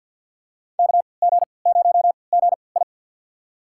Morse code, heard as a keyed tone, about 52 wpm, spaced as for 29 wpm, 690 Hz. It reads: KG0GI
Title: USA Call Signs - 5 Characters - 3x Character Spacing - 50wpm